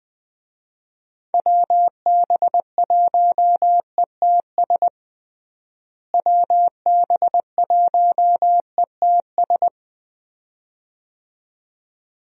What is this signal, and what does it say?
Morse code, heard as a keyed tone, about 20 wpm, 700 Hz. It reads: WB1ETS WB1ETS
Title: USA Call Signs - 6 Characters - Repeated 2x - 20wpm